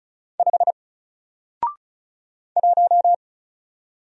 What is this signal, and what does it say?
Morse code, heard as a keyed tone, about 34 wpm, 700 Hz. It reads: 5 1
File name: Single-Letter-Number-35wpm